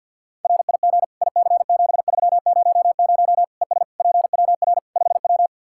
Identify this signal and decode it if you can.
Morse code, heard as a keyed tone, about 50 wpm, 700 Hz. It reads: RIG IC7300 ES PWR 5W
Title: Practice QSO - Exchange 2 - Info - 50wpm